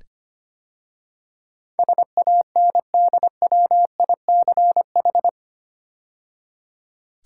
Morse code, heard as a keyed tone, about 25 wpm, 700 Hz. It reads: SANDWICH
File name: 8 Letter Words - First 3 Letters Given - 25wpm